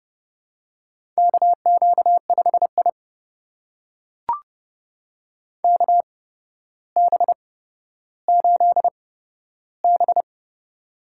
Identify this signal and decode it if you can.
Morse code, heard as a keyed tone, about 30 wpm, spaced as for 13 wpm, 700 Hz. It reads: KQ5I KB8B
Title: USA Call Signs - 8x Character Spacing - 30wpm